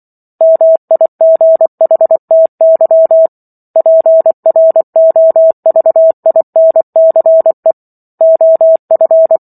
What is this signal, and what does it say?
Morse code, heard as a keyed tone, about 24 wpm, 650 Hz. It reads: MIGHTY PROVINCE OF
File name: Chapter 15 - 24 WPM